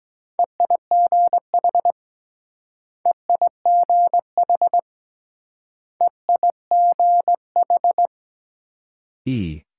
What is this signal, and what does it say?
Morse code, heard as a keyed tone, about 23 wpm, 700 Hz. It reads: EIGH EIGH EIGH